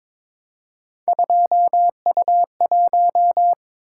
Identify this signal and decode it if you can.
Morse code, heard as a keyed tone, about 22 wpm, 700 Hz. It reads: 2U1
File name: practice / 22wpm / Three Letter-Number Combinations - 22wpm